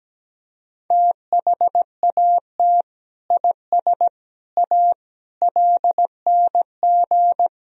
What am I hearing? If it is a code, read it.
Morse code, heard as a keyed tone, about 17 wpm, 700 Hz. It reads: THAT IS A LNG
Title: Sentences from Top 100 Words and Abbreviations 17wpm